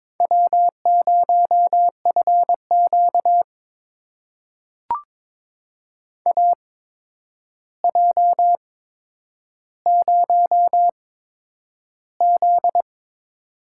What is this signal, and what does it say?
Morse code, heard as a keyed tone, about 22 wpm, spaced as for 9 wpm, 700 Hz. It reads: W0FQ AJ0Z
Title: USA Call Signs - 4 Characters - 8x Character Spacing - 22wpm